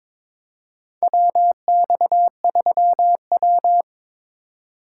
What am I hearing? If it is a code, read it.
Morse code, heard as a keyed tone, about 23 wpm, 700 Hz. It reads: WX3W
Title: USA Call Signs - 4 Characters - 3x Character Spacing - 22wpm